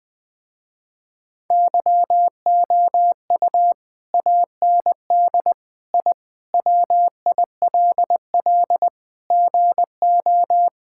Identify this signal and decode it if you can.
Morse code, heard as a keyed tone, about 20 wpm, 700 Hz. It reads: YOU AND I WILL GO
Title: Sentences with 2-Word Phrases - Spoken Beginning - 20wpm